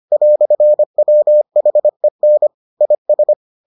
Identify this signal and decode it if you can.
Morse code, heard as a keyed tone, about 25 wpm, 600 Hz. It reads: "WHEN IS